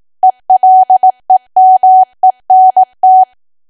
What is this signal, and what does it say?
Morse code, heard as a keyed tone, about 18 wpm, 750 Hz. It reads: ELEMENT
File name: cwa / W1AW-Level2 / 18WPM